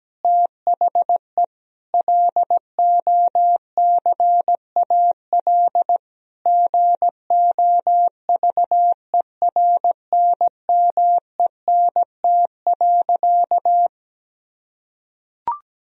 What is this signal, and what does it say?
Morse code, heard as a keyed tone, about 17 wpm, 700 Hz. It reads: THE LOCAL GOVERNMENT.
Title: Sentences from Top 1000 Words - Review Part 3 - 17wpm